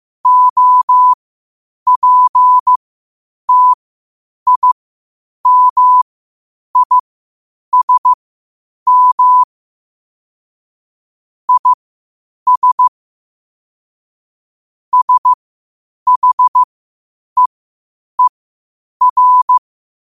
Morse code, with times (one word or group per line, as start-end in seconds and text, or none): 0.3-9.4 OPTIMISM
11.5-12.9 IS
14.9-19.6 SHEER